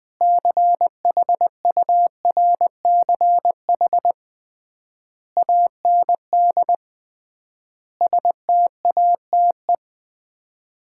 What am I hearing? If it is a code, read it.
Morse code, heard as a keyed tone, about 20 wpm, 700 Hz. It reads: CHURCH AND STATE
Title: 3x Word-Spacing - Sets of 5 Words - 20wpm